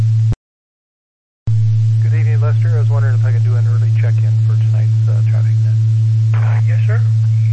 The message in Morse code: none

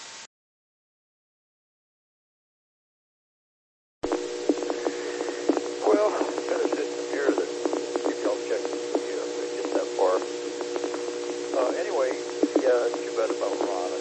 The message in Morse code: none

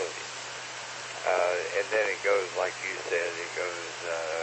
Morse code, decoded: none

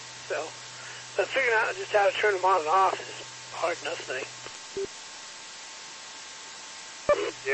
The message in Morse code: none

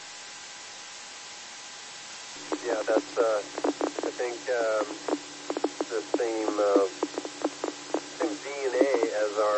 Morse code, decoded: none

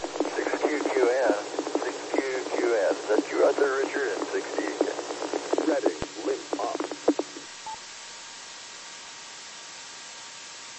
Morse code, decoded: none

